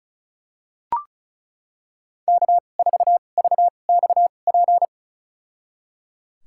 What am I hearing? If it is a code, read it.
Morse code, heard as a keyed tone, about 36 wpm, spaced as for 25 wpm, 700 Hz. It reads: K4VXP
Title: USA Call Signs - 5 Characters - 2x Character Spacing - 35wpm